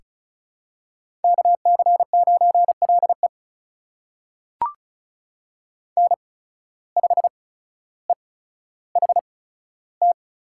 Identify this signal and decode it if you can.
Morse code, heard as a keyed tone, about 34 wpm, spaced as for 15 wpm, 700 Hz. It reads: KC9LE N5EHT
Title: USA Call Signs - 5 Characters - 8x Character Spacing - 35wpm